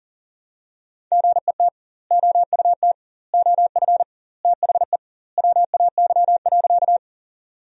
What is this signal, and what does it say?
Morse code, heard as a keyed tone, about 40 wpm, 700 Hz. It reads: GET OUT OF THE WAY.
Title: 2x Word-Spacing - Sentences from Top 200 Words - 40wpm